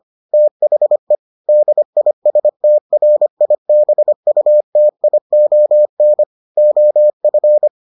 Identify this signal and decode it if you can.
Morse code, heard as a keyed tone, about 25 wpm, 600 Hz. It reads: THE DISTRIBUTION OF